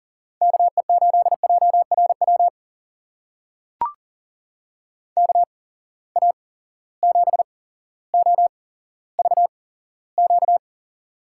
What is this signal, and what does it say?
Morse code, heard as a keyed tone, about 38 wpm, spaced as for 17 wpm, 700 Hz. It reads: KE8JRW KA7OVQ
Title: USA Call Signs - 6 Characters - 8x Character Spacing - 40wpm